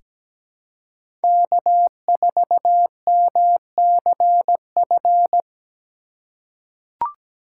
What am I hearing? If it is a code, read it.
Morse code, heard as a keyed tone, about 17 wpm, 700 Hz. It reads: K4MCF
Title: USA Call Signs - 5 Characters - 3x Character Spacing - 17wpm